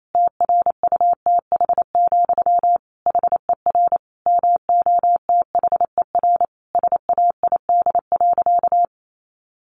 Morse code, heard as a keyed tone, about 28 wpm, 700 Hz. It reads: TRUTH, HER MOTHER SAID.
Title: Sentences from Top 800 Words 28wpm